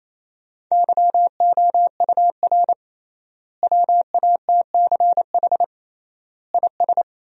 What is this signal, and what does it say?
Morse code, heard as a keyed tone, about 28 wpm, 700 Hz. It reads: YOUR WATCH IS
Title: 3x Word-Spacing - Sentences from Top 400 Words - 28wpm